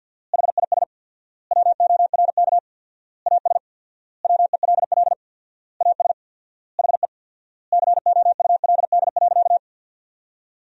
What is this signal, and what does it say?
Morse code, heard as a keyed tone, about 51 wpm, 700 Hz. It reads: HIS WORK AS WELL AS HE COULD.
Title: 4x Word-Spacing - Sentences from Top 500 Words - 50wpm